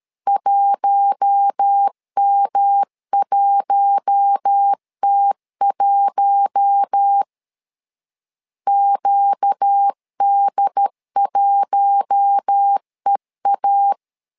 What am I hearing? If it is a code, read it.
Morse code, heard as a keyed tone, about 13 wpm, 800 Hz. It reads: 1M1T1 QD1EA
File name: LEZIONI / 4 Q1 12wpm 2028